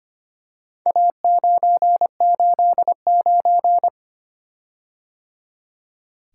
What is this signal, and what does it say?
Morse code, heard as a keyed tone, about 25 wpm, 700 Hz. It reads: A989
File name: practice / 25wpm / Four Letter-Number Combinations - 25wpm